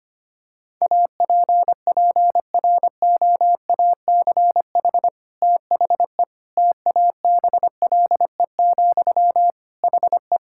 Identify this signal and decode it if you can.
Morse code, heard as a keyed tone, about 25 wpm, 700 Hz. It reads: APPROACH THE TABLE, HE
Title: Sentences from Top 1000 Words - Review Part 6 - 25wpm